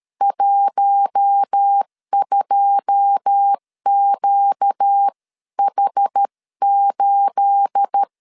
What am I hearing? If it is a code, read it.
Morse code, heard as a keyed tone, about 13 wpm, 800 Hz. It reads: 12QH8
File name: LEZIONI / 9 28 12wpm 2351